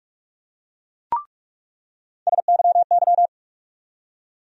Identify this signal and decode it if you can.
Morse code, heard as a keyed tone, about 45 wpm, 700 Hz. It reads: SYY
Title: USA Call Sign Postfixes - 3 Letters 45wpm